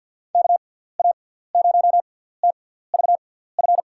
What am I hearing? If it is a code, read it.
Morse code, heard as a keyed tone, about 50 wpm, spaced as for 17 wpm, 700 Hz. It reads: KA0TVF
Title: USA Call Signs - 6 Characters - 6x Character Spacing - 50wpm